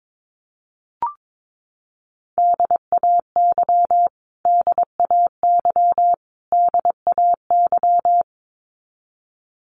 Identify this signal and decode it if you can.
Morse code, heard as a keyed tone, about 22 wpm, 700 Hz. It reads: DAY DAY DAY
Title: Top 500 Words Repeated 22wpm